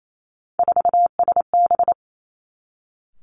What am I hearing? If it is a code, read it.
Morse code, heard as a keyed tone, about 28 wpm, 700 Hz. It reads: 4SB
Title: Three Challenging Letter-Number Combinations 28wpm